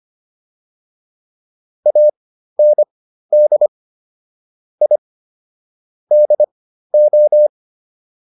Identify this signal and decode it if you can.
Morse code, heard as a keyed tone, about 25 wpm, spaced as for 13 wpm, 600 Hz. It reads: AND I DO